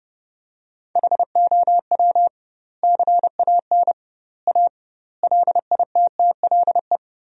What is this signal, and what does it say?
Morse code, heard as a keyed tone, about 30 wpm, 700 Hz. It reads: HOW CAN A LITTLE
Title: 2x Word-Spacing - Sentences from Top 500 Words - 30wpm